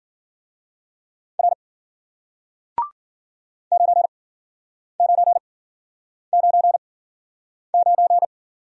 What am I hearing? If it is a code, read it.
Morse code, heard as a keyed tone, about 53 wpm, spaced as for 9 wpm, 700 Hz. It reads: R 9999